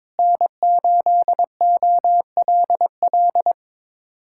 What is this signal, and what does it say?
Morse code, heard as a keyed tone, about 22 wpm, 700 Hz. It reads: N8OLL
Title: USA Callsign DE Callsign 22wpm